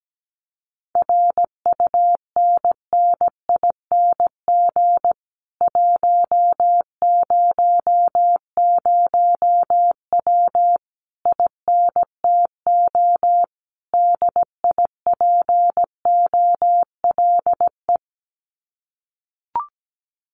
Morse code, heard as a keyed tone, about 17 wpm, 700 Hz. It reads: RUNNING 100W INTO DIPOLE
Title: Common QSO Phrases Encore 17wpm